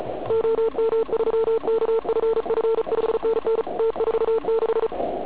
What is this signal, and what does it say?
Morse code, heard as a keyed tone, about 33 wpm, 450 Hz, with noise. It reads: OM3KFF 5NN T46
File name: Archiv / CUC106 / cuc106 PILEUP